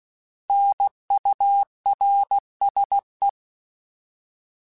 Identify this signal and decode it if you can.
Morse code, heard as a keyed tone, about 16 wpm, 800 Hz. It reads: NURSE